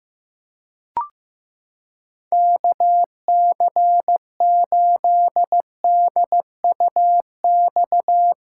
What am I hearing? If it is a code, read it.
Morse code, heard as a keyed tone, about 15 wpm, 700 Hz. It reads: KC8DUX
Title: USA Callsign DE Callsign 15wpm